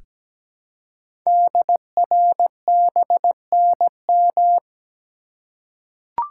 Morse code, heard as a keyed tone, about 17 wpm, 700 Hz. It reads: DRBNM
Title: Code Groups - 5 Letters - 17wpm